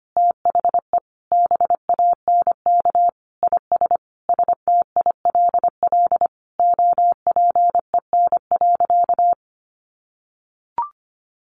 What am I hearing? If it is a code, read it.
Morse code, heard as a keyed tone, about 25 wpm, 700 Hz. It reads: THE BANK IS STILL OPEN.
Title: Sentences from Top 1500 Words 25wpm